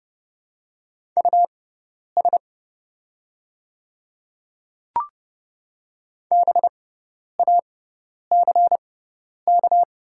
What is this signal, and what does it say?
Morse code, heard as a keyed tone, about 30 wpm, spaced as for 10 wpm, 700 Hz. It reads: US BACK
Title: 6x Character Spacing - Top 200 Words - 30wpm